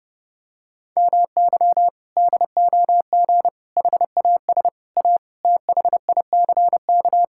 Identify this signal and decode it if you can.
Morse code, heard as a keyed tone, about 32 wpm, 700 Hz. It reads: MY DOG HAS A THICK
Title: Sentences from Top 1600 Words 30wpm